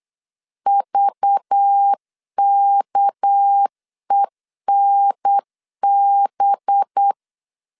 Morse code, heard as a keyed tone, about 8 wpm, 800 Hz. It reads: VKENB
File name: LEZIONI / 7 GF9Z / 7 GF9Z 8wpm 3213